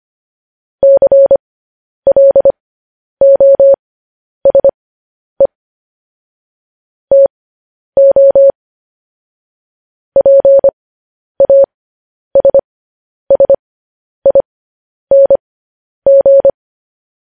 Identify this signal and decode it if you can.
Morse code, heard as a keyed tone, about 25 wpm, spaced as for 10 wpm, 550 Hz. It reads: CLOSE TO PASSING
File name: podcast / News headlines-1x-25x10